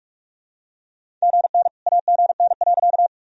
Morse code, heard as a keyed tone, about 44 wpm, 700 Hz. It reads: GN AGN.
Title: Sentences from Top 300 Words and Abbreviations 45wpm